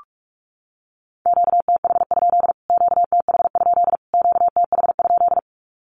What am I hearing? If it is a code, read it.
Morse code, heard as a keyed tone, about 44 wpm, 700 Hz. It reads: QTH? QTH? QTH?
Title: Q-Codes Repeated 3x 45wpm